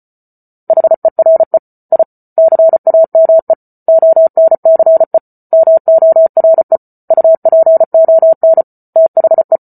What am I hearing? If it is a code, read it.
Morse code, heard as a keyed tone, about 34 wpm, 650 Hz. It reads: HERE I CAME ONCE MORE UPON THE